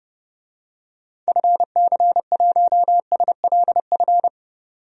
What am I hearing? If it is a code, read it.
Morse code, heard as a keyed tone, about 30 wpm, 700 Hz. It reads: FC1SLF
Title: DXCC-100 Call Signs - 6 Characters - 30wpm